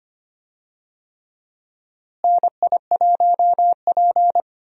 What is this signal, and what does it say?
Morse code, heard as a keyed tone, about 25 wpm, 700 Hz. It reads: NI1P
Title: USA Call Signs - 4 Characters - Repeated 3x - 25wpm